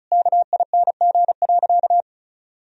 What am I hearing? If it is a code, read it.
Morse code, heard as a keyed tone, about 36 wpm, 700 Hz. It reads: KING.
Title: Sentences from Top 1000 Words - Review Part 2 - 35wpm